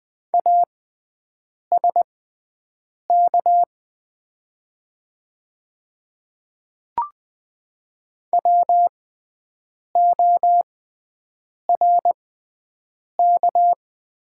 Morse code, heard as a keyed tone, about 20 wpm, spaced as for 7 wpm, 700 Hz. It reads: ASK WORK
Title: 6x Character Spacing - Top 500 Words - 20wpm